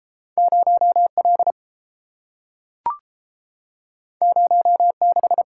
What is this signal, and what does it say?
Morse code, heard as a keyed tone, about 30 wpm, 700 Hz. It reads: 0 L 0 6